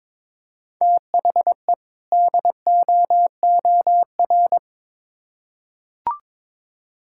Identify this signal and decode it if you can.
Morse code, heard as a keyed tone, about 22 wpm, 700 Hz. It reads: THE DOOR
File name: Build A Phrase - 2-Word Phrases - 22wpm